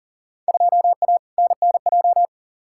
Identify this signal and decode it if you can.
Morse code, heard as a keyed tone, about 40 wpm, 700 Hz. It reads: 2A NNJ
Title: Field Day Exchanges - 40wpm